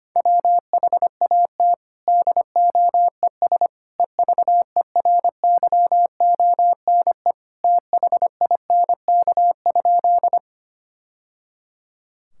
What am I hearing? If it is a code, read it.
Morse code, heard as a keyed tone, about 25 wpm, 700 Hz. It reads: WHAT DOES EVERYONE THINK?
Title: Sentences from Top 1000 Words - Review Part 5 - 25wpm